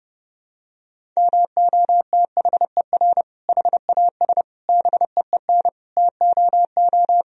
Morse code, heard as a keyed tone, about 30 wpm, 700 Hz. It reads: MOTHER HAS BEEN TOO
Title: Sentences from Top 1000 Words - Review Part 1 - 30wpm